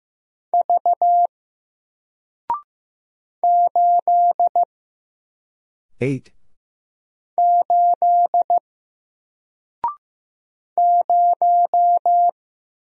Morse code, heard as a keyed tone, about 15 wpm, 700 Hz. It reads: V 8 8 0
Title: Single Character (Letter:Number:Punctuation:Prosign) 15wpm